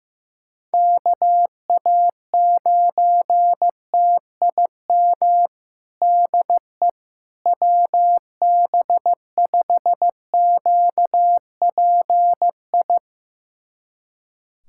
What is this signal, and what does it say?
Morse code, heard as a keyed tone, about 16 wpm, 700 Hz. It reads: KA9TIM DE WB5QPI
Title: USA Callsign DE Callsign - 6 Characters - 15wpm